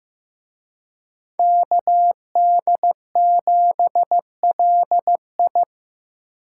Morse code, heard as a keyed tone, about 17 wpm, 700 Hz. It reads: KD7LI
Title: Speed Racing - USA 2x2 Call Signs 15wpm